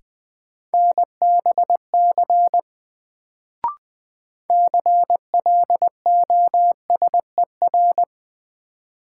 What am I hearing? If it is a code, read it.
Morse code, heard as a keyed tone, about 20 wpm, 700 Hz. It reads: NBC CLOSER
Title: Sets of 5 Words Encore - New Word Drill - 20wpm